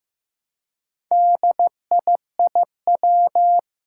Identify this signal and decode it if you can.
Morse code, heard as a keyed tone, about 15 wpm, 700 Hz. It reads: DIIW
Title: Four Challenging Letter-Number Combinations 15wpm